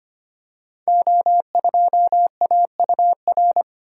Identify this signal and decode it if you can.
Morse code, heard as a keyed tone, about 25 wpm, 690 Hz. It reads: O2AUR
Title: Code Groups - 5 Letter:Numbers - 25wpm